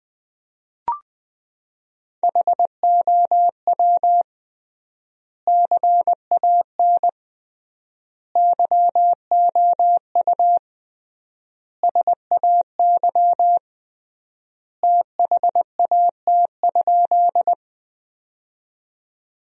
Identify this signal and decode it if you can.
Morse code, heard as a keyed tone, about 20 wpm, 700 Hz. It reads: HOW CAN YOU SAY THAT?
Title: 3x Word-Spacing - Sentences from Top 200 Words - 20wpm